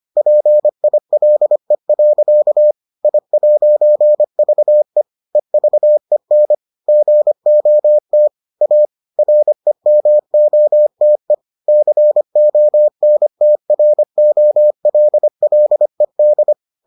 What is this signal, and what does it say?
Morse code, heard as a keyed tone, about 25 wpm, 600 Hz. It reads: PILE. I'VE EVEN GOT A REMOTE CONTROLLED